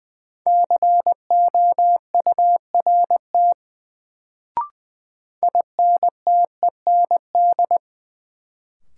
Morse code, heard as a keyed tone, about 20 wpm, 700 Hz. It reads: COURT INTEND
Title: Sentences from Top 1600 Words - New Word Drill - 20wpm